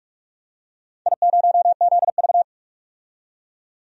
Morse code, heard as a keyed tone, about 46 wpm, 700 Hz. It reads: I0ZV
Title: DXCC-100 Call Signs - 4 Characters - 45wpm